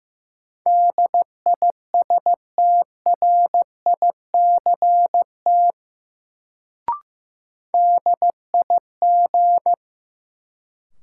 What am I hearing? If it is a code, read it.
Morse code, heard as a keyed tone, about 15 wpm, 700 Hz. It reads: DISTRICT DIG
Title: Sentences from Top 1600 Words - New Word Drill - 15wpm